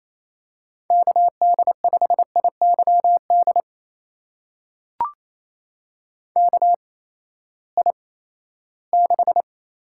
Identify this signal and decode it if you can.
Morse code, heard as a keyed tone, about 28 wpm, spaced as for 16 wpm, 700 Hz. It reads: KD5IYD KI6
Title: USA Call Signs - 6 Characters - 8x Character Spacing - 28wpm